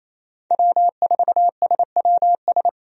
Morse code, heard as a keyed tone, about 28 wpm, 700 Hz. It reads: W4SWS
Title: USA Call Signs - 5 Characters - 28wpm